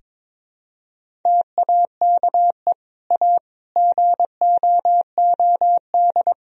Morse code, heard as a keyed tone, about 22 wpm, 700 Hz. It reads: TAKE A GOOD
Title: Sentences from Top 1000 Words - Review Part 3 - 22wpm